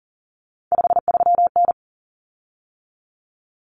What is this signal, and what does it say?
Morse code, heard as a keyed tone, about 40 wpm, 700 Hz. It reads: 53N
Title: Common Cut Numbers - 3-Digit 40wpm